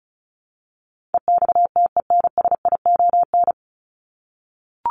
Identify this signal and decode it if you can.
Morse code, heard as a keyed tone, about 34 wpm, 700 Hz. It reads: EXTENSION
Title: Top 1501-1750 Words 35wpm